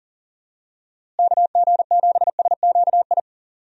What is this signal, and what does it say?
Morse code, heard as a keyed tone, about 40 wpm, 700 Hz. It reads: KG7SQI
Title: USA Call Signs - 6 Characters - Repeated 3x - 40wpm